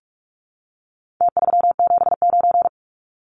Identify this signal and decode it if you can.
Morse code, heard as a keyed tone, about 46 wpm, 700 Hz. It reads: T379
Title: Numbers - Contest Style - 4-Digit 45wpm